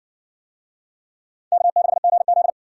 Morse code, heard as a keyed tone, about 58 wpm, 700 Hz. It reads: K6GZ